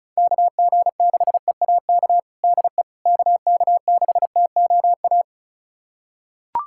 Morse code, heard as a keyed tone, about 34 wpm, 700 Hz. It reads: KG6EAK DE KK6TOA
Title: USA Callsign DE Callsign - 6 Characters - 35wpm